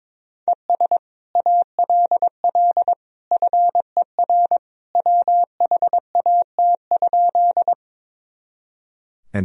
Morse code, heard as a keyed tone, about 22 wpm, 700 Hz. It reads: ES ALL FER WHAT?
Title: Sentences from Top 100 Words and Abbreviations 22wpm